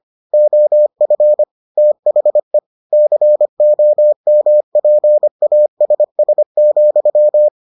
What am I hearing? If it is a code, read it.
Morse code, heard as a keyed tone, about 25 wpm, 600 Hz. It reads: OF THE COMPASS,